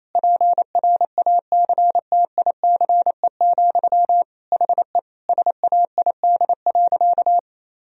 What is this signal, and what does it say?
Morse code, heard as a keyed tone, about 28 wpm, 700 Hz. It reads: PRACTICE, HE SAID.
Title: Sentences from Top 1000 Words - Review Part 4 - 28wpm